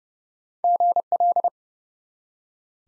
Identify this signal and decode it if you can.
Morse code, heard as a keyed tone, about 30 wpm, 700 Hz. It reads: GL
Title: Top 90 Abbr 30WPM